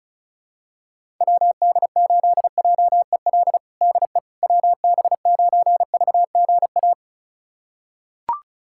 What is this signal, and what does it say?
Morse code, heard as a keyed tone, about 36 wpm, 700 Hz. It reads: WD8JEL DE WB9VGA
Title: USA Callsign DE Callsign - 6 Characters - 35wpm